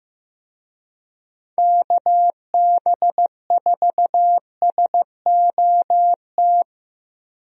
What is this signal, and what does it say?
Morse code, heard as a keyed tone, about 15 wpm, 700 Hz. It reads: KB4SOT
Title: USA Call Signs - 6 Characters - Repeated 2x - 15wpm